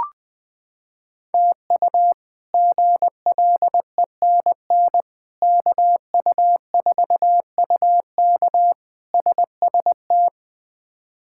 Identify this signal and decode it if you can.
Morse code, heard as a keyed tone, about 20 wpm, 700 Hz. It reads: TU GLENN KU4UK SST
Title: K1USN SST QSO Practice - 20wpm